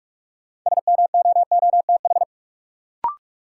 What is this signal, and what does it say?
Morse code, heard as a keyed tone, about 45 wpm, 700 Hz. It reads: SMOOTH
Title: Sentences from Top 1400 Words - New Word Drill Alt - 45wpm